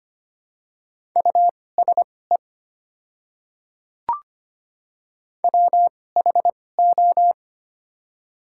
Morse code, heard as a keyed tone, about 25 wpm, spaced as for 18 wpm, 690 Hz. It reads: USE WHO
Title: 2x Character Spacing - Top 100 Words - 25wpm